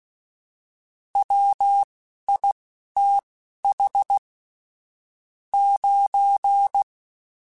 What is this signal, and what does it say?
Morse code, heard as a keyed tone, about 16 wpm, spaced as for 11 wpm, 800 Hz. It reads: WITH 9
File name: Technical Subjects / Technical Subjects-mp3s / AND Gate @10